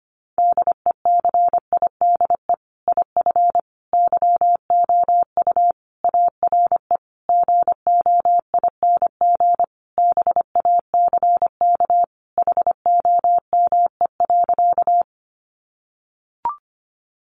Morse code, heard as a keyed tone, about 25 wpm, 700 Hz. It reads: DECIDE IF YOU ARE GOING BACK HOME.
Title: Sentences from Top 500 Words 25wpm